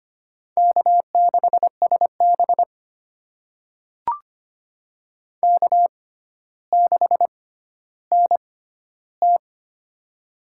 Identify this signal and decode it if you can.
Morse code, heard as a keyed tone, about 25 wpm, spaced as for 13 wpm, 700 Hz. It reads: K6SB K6NT
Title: USA Call Signs - 6x Character Spacing - 25wpm